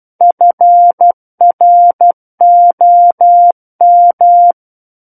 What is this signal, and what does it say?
Morse code, heard as a keyed tone, about 12 wpm, 700 Hz. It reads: FROM
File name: Book One / 03-Chapter 3 - 12 WPM